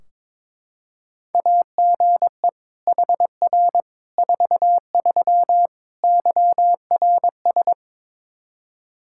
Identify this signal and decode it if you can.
Morse code, heard as a keyed tone, about 22 wpm, 700 Hz. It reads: AGE HR 43 YRS
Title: Practice QSO - Exchange 3 - More Info - 22wpm